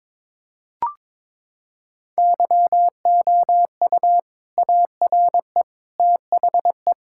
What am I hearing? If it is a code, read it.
Morse code, heard as a keyed tone, about 22 wpm, 700 Hz. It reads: YOU ARE THE
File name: practice / 22wpm / Sentences with 2-Word Phrases - 22wpm